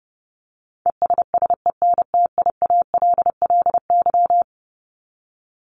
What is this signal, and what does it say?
Morse code, heard as a keyed tone, about 30 wpm, 700 Hz. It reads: ESSENTIALLY
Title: LY Suffix - Top 100 Words with Base Words 30wpm